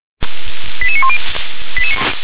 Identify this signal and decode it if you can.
Five-tone selective call (ZVEI), heard as quick runs of five tones, tones of about 80 ms. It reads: 88910